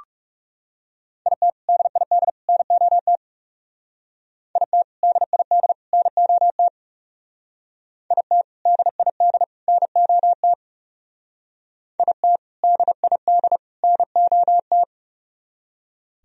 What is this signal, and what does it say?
Morse code, heard as a keyed tone, about 38 wpm, 700 Hz. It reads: IT DID NOT IT DID NOT IT DID NOT IT DID NOT